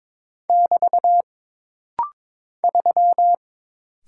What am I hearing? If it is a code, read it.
Morse code, heard as a keyed tone, about 22 wpm, 700 Hz. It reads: <BT> 3
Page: This rapid-fire timing is geared to push your ICR (Instant Character Recognition) skills.